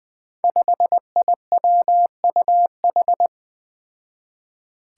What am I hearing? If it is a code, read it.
Morse code, heard as a keyed tone, about 20 wpm, 700 Hz. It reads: 5IWUH
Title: Five Challenging Letter-Number Combinations 20wpm